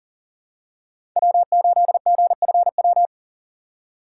Morse code, heard as a keyed tone, about 37 wpm, 700 Hz. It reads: W8GFW
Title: USA Call Signs - 5 Characters - 4x Character Spacing - 40wpm